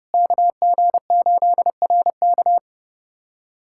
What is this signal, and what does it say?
Morse code, heard as a keyed tone, about 30 wpm, 700 Hz. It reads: KG8RK
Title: USA Callsign DE Callsign 30wpm